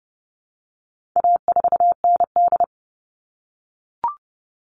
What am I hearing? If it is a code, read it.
Morse code, heard as a keyed tone, about 30 wpm, 700 Hz. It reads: A4ND
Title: All Cut Numbers - 4-Digit 30wpm